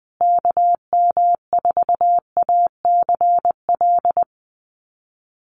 Morse code, heard as a keyed tone, about 19 wpm, 700 Hz. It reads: KM4ACL
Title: USA Call Signs Repeated 3x 20wpm